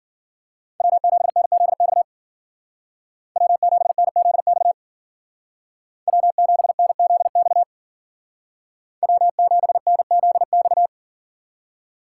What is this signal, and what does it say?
Morse code, heard as a keyed tone, about 47 wpm, 700 Hz. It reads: W7NZX W7NZX W7NZX W7NZX
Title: Speed Racing - USA Call Signs 40wpm